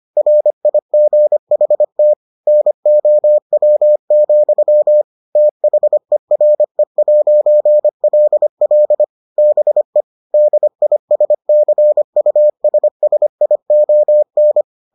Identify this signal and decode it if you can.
Morse code, heard as a keyed tone, about 25 wpm, 600 Hz. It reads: RIGHT NOW, THERE'LL BE DISCUSSION